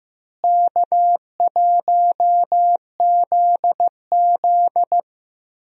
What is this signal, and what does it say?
Morse code, heard as a keyed tone, about 16 wpm, 700 Hz. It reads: K1ZZ
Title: USA Call Signs - 4 Characters - Repeated 3x - 15wpm